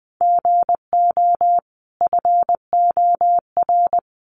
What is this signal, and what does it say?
Morse code, heard as a keyed tone, about 19 wpm, 700 Hz. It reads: GO FOR
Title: Sentences from Top 1100 Words 20wpm